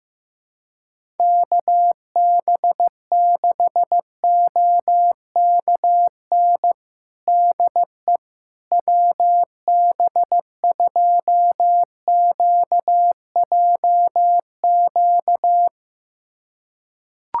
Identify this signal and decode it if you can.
Morse code, heard as a keyed tone, about 15 wpm, 700 Hz. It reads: KB6OKN DE WB2QJQ
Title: USA Callsign DE Callsign - 6 Characters - 15wpm